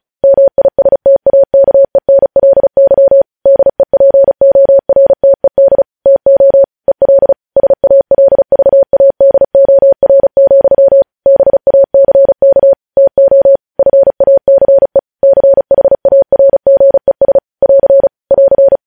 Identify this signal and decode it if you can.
Morse code, heard as a keyed tone, about 36 wpm, 550 Hz. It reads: MISTAKENLY DEPORTED TO EL SALVADOR, BACK TO FACE CHARGES <AR> <AR>
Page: News Headlines in Morse Code at 35 WPM